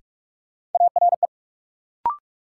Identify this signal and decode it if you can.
Morse code, heard as a keyed tone, about 42 wpm, 700 Hz. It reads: ARE
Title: Sentences from Top 100 Words - New Word Drill - 45wpm